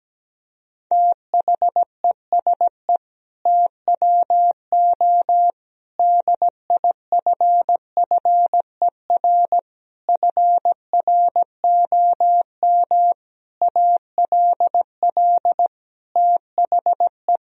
Morse code, heard as a keyed tone, about 17 wpm, 700 Hz. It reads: THESE TWO DIFFER FROM ALL THE
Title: Sentences from Top 1000 Words - Review Part 1 - 17wpm